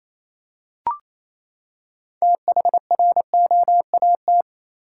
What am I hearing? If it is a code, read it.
Morse code, heard as a keyed tone, about 28 wpm, 700 Hz. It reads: THROAT
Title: Six Letter Words - 28wpm